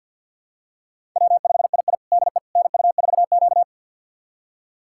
Morse code, heard as a keyed tone, about 51 wpm, 700 Hz. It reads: W5II DE NU4Q
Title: USA Callsign DE Callsign - 4 Characters - 50wpm